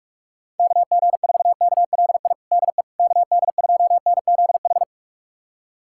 Morse code, heard as a keyed tone, about 41 wpm, 700 Hz. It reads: KG4KLI DE KD2NZH
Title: USA Callsign DE Callsign 45wpm